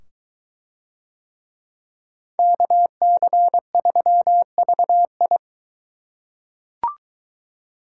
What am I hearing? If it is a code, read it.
Morse code, heard as a keyed tone, about 22 wpm, 700 Hz. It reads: KC3VI